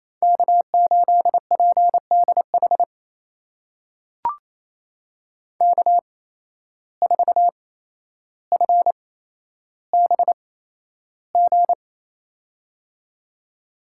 Morse code, heard as a keyed tone, about 28 wpm, spaced as for 12 wpm, 700 Hz. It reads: K8PDH K4FBG
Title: USA Call Signs - 5 Characters - 8x Character Spacing - 28wpm